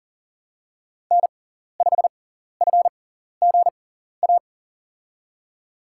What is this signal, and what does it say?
Morse code, heard as a keyed tone, about 37 wpm, spaced as for 14 wpm, 700 Hz. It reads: N5FGA
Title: USA Call Signs - 5 Characters - 6x Character Spacing - 40wpm